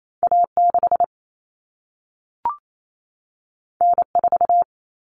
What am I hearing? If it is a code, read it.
Morse code, heard as a keyed tone, about 28 wpm, 700 Hz. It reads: A6 N4
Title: Common Cut Numbers - 2-Digit 28wpm